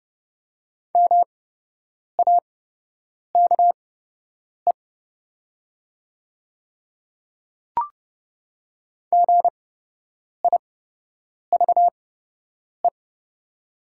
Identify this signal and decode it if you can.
Morse code, heard as a keyed tone, about 30 wpm, spaced as for 8 wpm, 700 Hz. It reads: MAKE GIVE
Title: 8x Character Spacing - Top 200 Words - 30wpm